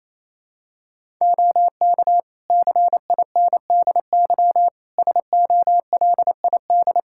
Morse code, heard as a keyed tone, about 28 wpm, 700 Hz. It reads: OK CINDY SOLID
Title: K1USN SST QSO Practice 28wpm